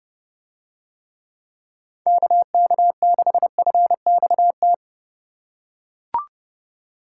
Morse code, heard as a keyed tone, about 29 wpm, 700 Hz. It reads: KK6FXT